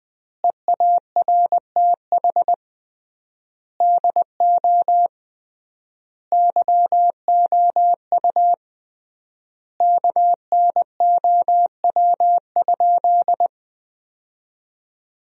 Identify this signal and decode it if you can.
Morse code, heard as a keyed tone, about 20 wpm, 700 Hz. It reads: EARTH DO YOU KNOW?
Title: 3x Word-Spacing - Sentences from Top 200 Words - 20wpm